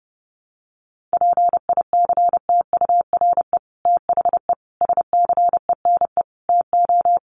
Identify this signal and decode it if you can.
Morse code, heard as a keyed tone, about 30 wpm, 700 Hz. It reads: PICTURE THE SCENE TO
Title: Sentences from Top 1100 Words 30wpm